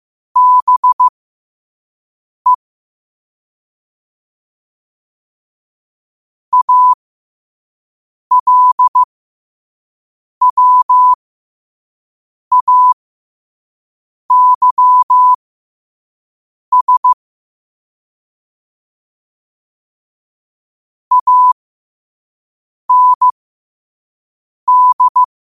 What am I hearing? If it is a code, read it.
Morse code, heard as a keyed tone, about 15 wpm, spaced as for 5 wpm, 1000 Hz. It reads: BE ALWAYS AND
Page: New quotes every day in morse code at 5 Words per minute.